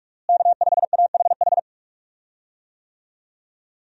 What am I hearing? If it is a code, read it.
Morse code, heard as a keyed tone, about 42 wpm, 700 Hz. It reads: K5AHH
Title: USA Call Signs - 5 Characters - Repeated 3x - 45wpm